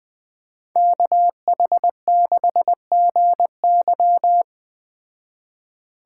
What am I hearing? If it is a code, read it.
Morse code, heard as a keyed tone, about 20 wpm, 700 Hz. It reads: KH6GY
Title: Speed Racing - USA 2x2 Call Signs 17wpm